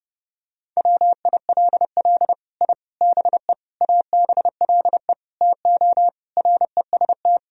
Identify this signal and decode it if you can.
Morse code, heard as a keyed tone, about 32 wpm, 700 Hz. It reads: WILL I BE ABLE TO REST
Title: Sentences from Top 1000 Words - Review Part 3 - 30wpm